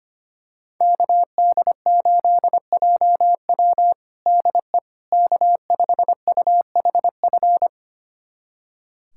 Morse code, heard as a keyed tone, about 25 wpm, 700 Hz. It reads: KD8JW DE K5UHF
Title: USA Callsign DE Callsign - 5 Characters - 25wpm